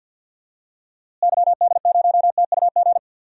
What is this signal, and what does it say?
Morse code, heard as a keyed tone, about 53 wpm, 700 Hz. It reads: YD0TUG
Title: DXCC-100 Call Signs - 6 Characters - Repeated 2x - 50wpm